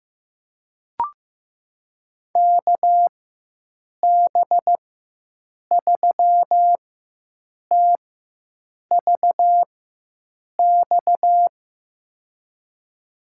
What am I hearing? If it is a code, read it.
Morse code, heard as a keyed tone, about 15 wpm, spaced as for 7 wpm, 700 Hz. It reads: KB3TVX
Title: USA Call Signs - 6 Characters - 4x Character Spacing - 15wpm